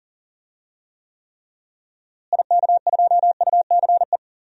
Morse code, heard as a keyed tone, about 40 wpm, 700 Hz. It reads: IK2UCE
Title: DXCC-100 Call Signs - 6 Characters - Repeated 3x - 40wpm